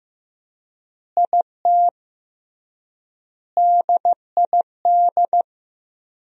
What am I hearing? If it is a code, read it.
Morse code, heard as a keyed tone, about 15 wpm, 700 Hz. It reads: IT DID
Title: 3x Word-Spacing - Sentences from Top 200 Words - 15wpm